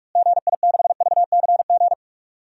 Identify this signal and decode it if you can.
Morse code, heard as a keyed tone, about 41 wpm, 700 Hz. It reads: GI6VCG
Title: DXCC-100 Call Signs - 6 Characters - Repeated 2x - 45wpm